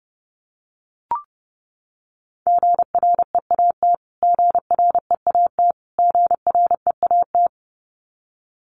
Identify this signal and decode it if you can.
Morse code, heard as a keyed tone, about 30 wpm, 700 Hz. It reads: GREAT GREAT GREAT
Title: Top 200 Words Repeated 30wpm